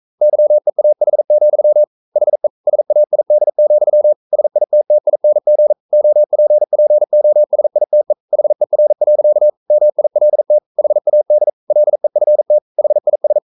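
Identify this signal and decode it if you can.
Morse code, heard as a keyed tone, about 42 wpm, 600 Hz. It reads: YEAH, HE SAID, SITTING OPPOSITE HER. MILT HAD LEFT HIS
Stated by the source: Morse code practice at 41 wpm, from CW med Gnister.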